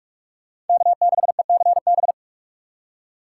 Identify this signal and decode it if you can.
Morse code, heard as a keyed tone, about 40 wpm, 700 Hz. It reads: K6ECB